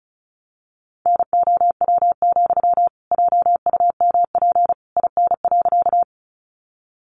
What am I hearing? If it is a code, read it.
Morse code, heard as a keyed tone, about 34 wpm, 700 Hz. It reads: NOW, JUMP IN.
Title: Sentences from Top 1300 Words 35wpm